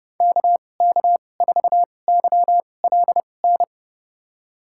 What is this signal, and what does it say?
Morse code, heard as a keyed tone, about 32 wpm, spaced as for 22 wpm, 700 Hz. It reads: KK4YLN
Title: USA Call Signs - 6 Characters - 2x Character Spacing - 30wpm